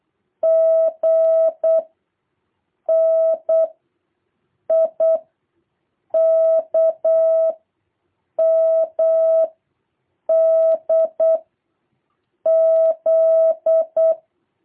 Morse code, heard as a keyed tone, about 8 wpm, 650 Hz. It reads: G N I K M D Z
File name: jh8cbh / 2023cw